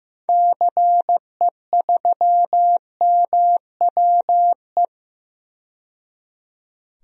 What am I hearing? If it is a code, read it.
Morse code, heard as a keyed tone, about 15 wpm, 700 Hz. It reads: CE3MWE
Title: DXCC-100 Call Signs - 6 Characters - 15wpm